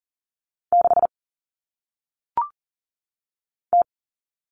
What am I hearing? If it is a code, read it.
Morse code, heard as a keyed tone, about 40 wpm, 700 Hz. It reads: 6 T
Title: All Cut Numbers - 1-Digit 40wpm